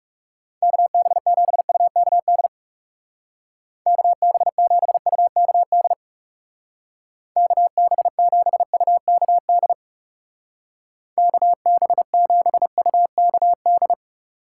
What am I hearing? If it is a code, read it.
Morse code, heard as a keyed tone, about 36 wpm, 700 Hz. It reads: KB7UKD KB7UKD KB7UKD KB7UKD
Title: Speed Racing - USA 2x3 Call Signs 30wpm